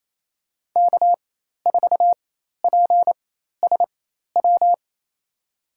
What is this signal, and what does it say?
Morse code, heard as a keyed tone, about 28 wpm, spaced as for 13 wpm, 700 Hz. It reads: K4PSW
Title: USA Call Signs - 4x Character Spacing - 28wpm